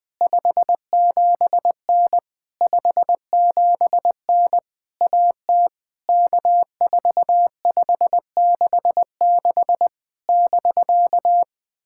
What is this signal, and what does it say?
Morse code, heard as a keyed tone, about 21 wpm, 700 Hz. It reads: 57N 57N AT K4566 <BK>
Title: POTA QSO Practice - 20wpm